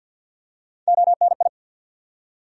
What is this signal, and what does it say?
Morse code, heard as a keyed tone, about 48 wpm, 700 Hz. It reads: ONI
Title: USA Call Sign Postfixes - 3 Letters 50wpm